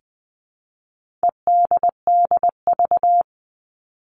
Morse code, heard as a keyed tone, about 20 wpm, 700 Hz. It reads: EDDV
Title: All Cut Numbers - 4-Digit 20wpm